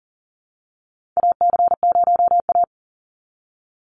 Morse code, heard as a keyed tone, about 40 wpm, 700 Hz. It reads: AC0A
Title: NA SOTA Chaser Callsigns 40wpm